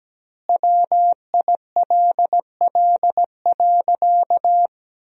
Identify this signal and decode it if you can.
Morse code, heard as a keyed tone, about 17 wpm, 700 Hz. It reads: WILL.
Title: 2x Word-Spacing - Sentences from Top 100 Words - 17wpm